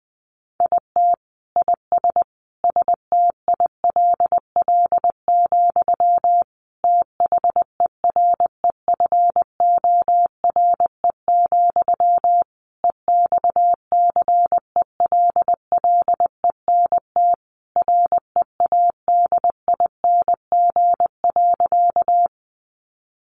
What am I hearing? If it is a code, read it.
Morse code, heard as a keyed tone, about 20 wpm, 700 Hz. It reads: IT IS STILL, THEREFORE, EXCELLENT READING.
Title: Sentences from Top 1000 Words 20wpm